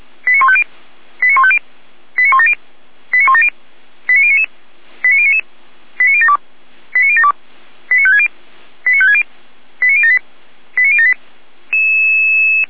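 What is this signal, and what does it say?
Five-tone selective call (ZVEI), heard as quick runs of five tones, tones of about 70 ms. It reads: 89150 89150 89170 89170 89099 89099 89072 89072 89560 89560 89078 89078
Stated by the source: ZVEI five-tone call received by SDR.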